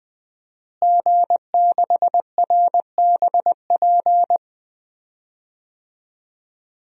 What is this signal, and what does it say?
Morse code, heard as a keyed tone, about 20 wpm, 700 Hz. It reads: G6RBP
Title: DXCC-100 Call Signs - 5 Characters - Repeated 2x - 20wpm